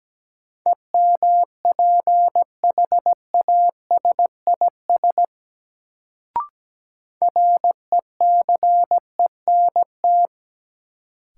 Morse code, heard as a keyed tone, about 17 wpm, 700 Hz. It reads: EMPHASIS RECENT
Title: Sentences from Top 1400 Words - New Word Drill Alt - 17wpm